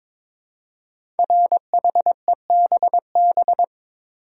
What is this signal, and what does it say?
Morse code, heard as a keyed tone, about 22 wpm, 690 Hz. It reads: RHEBB
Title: Code Groups - 5 Letters - 22wpm